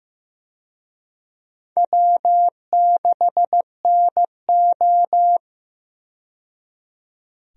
Morse code, heard as a keyed tone, about 15 wpm, 700 Hz. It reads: W6NO
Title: USA Call Signs - 4 Characters - 15wpm